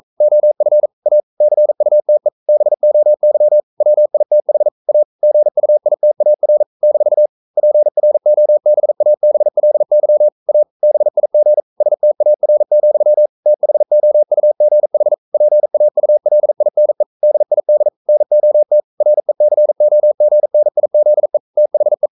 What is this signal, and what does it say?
Morse code, heard as a keyed tone, about 42 wpm, 600 Hz. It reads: OF A CUTE BOY WITH A GUITAR - PROBABLY A BIG STAR, THOUGH PAULINE DID NOT RECOGNIZE THE
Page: Romanen Never af Ken Follet på engelsk.